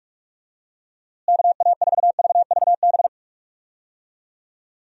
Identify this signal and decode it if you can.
Morse code, heard as a keyed tone, about 44 wpm, 700 Hz. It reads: KA4VVB
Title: USA Call Signs - 6 Characters - Repeated 2x - 45wpm